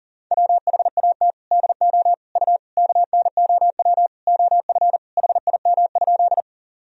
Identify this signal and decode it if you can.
Morse code, heard as a keyed tone, about 40 wpm, 700 Hz. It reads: WHAT DO U KNOW OF HIM?
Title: Sentences from Top 100 Words and Abbreviations 40wpm